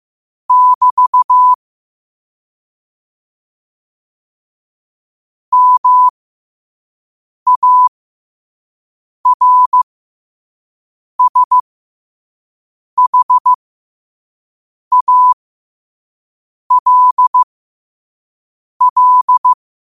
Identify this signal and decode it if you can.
Morse code, heard as a keyed tone, about 15 wpm, spaced as for 5 wpm, 1000 Hz. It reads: <BT> MARSHALL
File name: qotdfiles / 5wpm